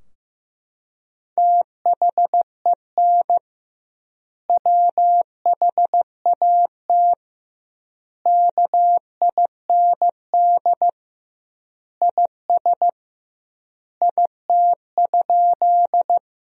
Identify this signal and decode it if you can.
Morse code, heard as a keyed tone, about 15 wpm, 700 Hz. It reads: THEN WHAT KIND IS IT?
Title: 2x Word-Spacing - Sentences from Top 200 Words - 15wpm